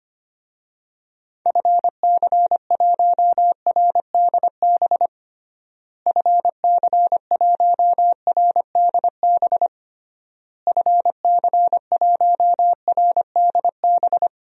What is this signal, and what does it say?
Morse code, heard as a keyed tone, about 25 wpm, 700 Hz. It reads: FC1RDB FC1RDB FC1RDB
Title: DXCC-100 Call Signs - 6 Characters - Repeated 3x - 25wpm